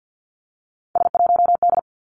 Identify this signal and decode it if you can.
Morse code, heard as a keyed tone, about 50 wpm, 700 Hz. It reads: S1D
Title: Three Challenging Letter-Number Combinations 50wpm